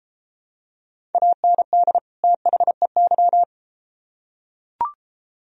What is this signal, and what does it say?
Morse code, heard as a keyed tone, about 30 wpm, 700 Hz. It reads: AND THEY